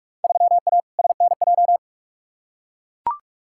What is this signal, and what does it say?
Morse code, heard as a keyed tone, about 41 wpm, 700 Hz. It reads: 3A SNJ
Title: Field Day Exchanges - 45wpm